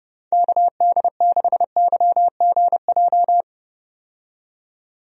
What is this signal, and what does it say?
Morse code, heard as a keyed tone, about 30 wpm, 700 Hz. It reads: KD6YGJ
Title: USA Call Signs - 6 Characters - 30wpm